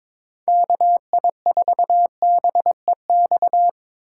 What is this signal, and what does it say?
Morse code, heard as a keyed tone, about 22 wpm, 700 Hz. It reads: KI4BEX
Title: USA Call Signs - 6 Characters - 4x Character Spacing - 22wpm